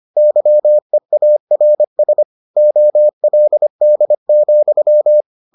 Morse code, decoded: YEARS OLD,

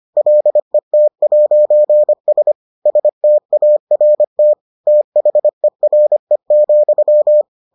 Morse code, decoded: LET'S START THERE,

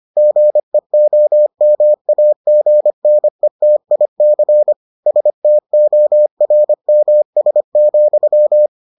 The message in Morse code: GEOMAGNETIC STORMS,

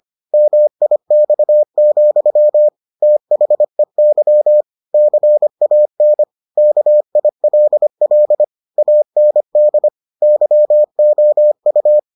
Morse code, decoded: MIX, THEY CAN KILL AND YOU

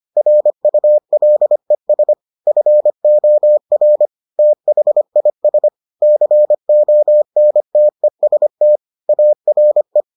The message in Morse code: RULES FOR THIS CONTEST ARE

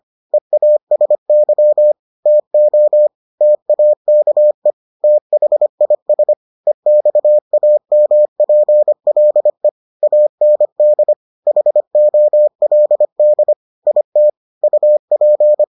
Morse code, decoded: EASY TO TAKE THIS EXAMPLE AND HOLD IT UP

